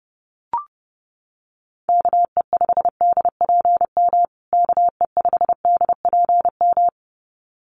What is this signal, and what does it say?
Morse code, heard as a keyed tone, about 30 wpm, 700 Hz. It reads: KE5DPM KE5DPM
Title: USA Call Signs Repeated 2x 30wpm